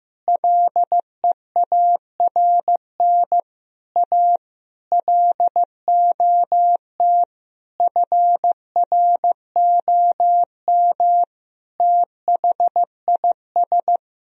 Morse code, heard as a keyed tone, about 15 wpm, 700 Hz. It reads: LEARN A LOT FROM THIS
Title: Sentences from Top 1000 Words - Review Part 2 - 15wpm